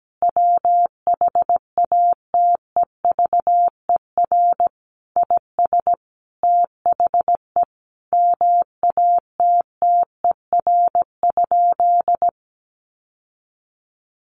Morse code, heard as a keyed tone, about 17 wpm, 700 Hz. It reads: WHATEVER IS THE MATTER?
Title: Sentences from Top 1100 Words 17wpm